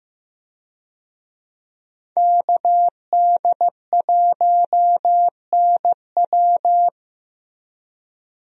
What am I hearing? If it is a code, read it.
Morse code, heard as a keyed tone, about 15 wpm, 700 Hz. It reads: KD1NW
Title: USA Call Signs - 5 Characters - Repeated 3x - 15wpm